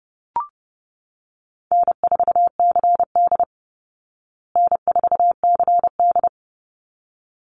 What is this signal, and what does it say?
Morse code, heard as a keyed tone, about 30 wpm, 700 Hz. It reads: N4CD N4CD
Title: Call Signs Repeated 3x - 30wpm